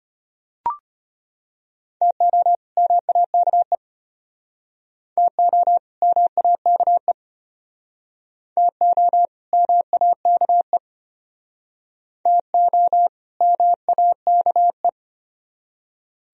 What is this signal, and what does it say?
Morse code, heard as a keyed tone, about 34 wpm, 700 Hz. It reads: TO MAKE TO MAKE TO MAKE TO MAKE